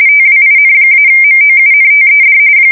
RTTY, about 45.45 baud. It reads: TWIN-MOUNTED
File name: fsk170 / battleshipnj